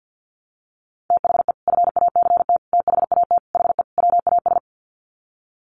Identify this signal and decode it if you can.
Morse code, heard as a keyed tone, about 50 wpm, 700 Hz. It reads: THE FACT THAT HE WAS
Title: Sets of 5 Words 50wpm